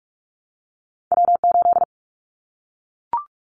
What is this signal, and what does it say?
Morse code, heard as a keyed tone, about 44 wpm, 700 Hz. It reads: W8
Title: Callsign Prefixes for Top 25 DXCC - 45wpm